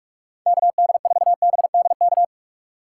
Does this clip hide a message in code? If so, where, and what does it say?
Yes: Morse code from 0.5 to 2.2 s, heard as a keyed tone, about 45 wpm, 700 Hz. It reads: KD4BDK